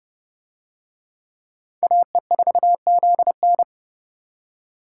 Morse code, heard as a keyed tone, about 26 wpm, 700 Hz. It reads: AE4ZN